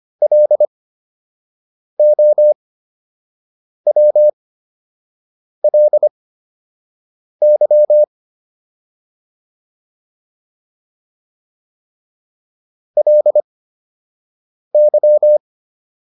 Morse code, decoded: LOWLY LY

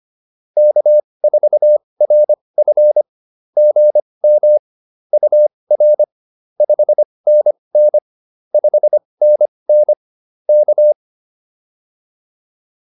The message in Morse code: K4RF GM UR 5NN 5NN K

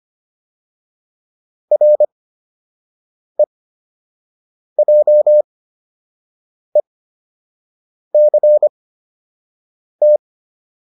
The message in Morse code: REJECT

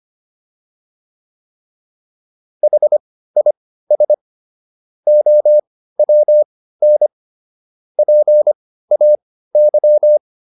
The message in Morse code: HIS OWN PAY

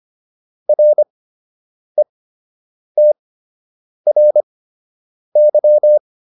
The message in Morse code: RETRY

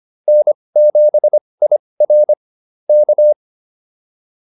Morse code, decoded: N7IR K

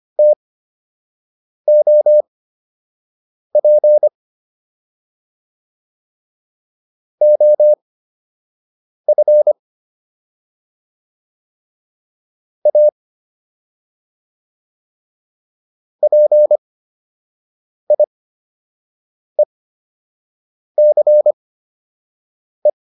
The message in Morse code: TOP OF A PIECE

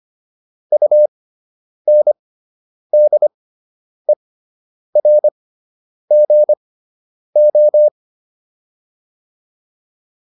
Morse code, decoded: UNDERGO

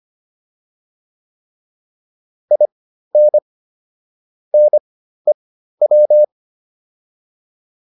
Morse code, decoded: IN NEW

